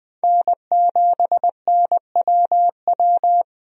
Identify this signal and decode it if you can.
Morse code, heard as a keyed tone, about 20 wpm, 700 Hz. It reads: N7NWW
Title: USA Call Signs - 5 Characters - 20wpm